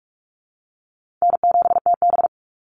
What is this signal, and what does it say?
Morse code, heard as a keyed tone, about 44 wpm, 700 Hz. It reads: N7TB
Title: Call Signs Repeated 3x - 45wpm